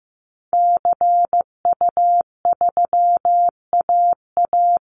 Morse code, heard as a keyed tone, about 15 wpm, 700 Hz. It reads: CU3AA
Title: Call Signs Repeated 3x - 15wpm